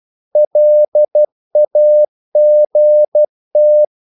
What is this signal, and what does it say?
Morse code, heard as a keyed tone, about 12 wpm, 600 Hz. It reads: LAGT
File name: Korte ord DK 12wpm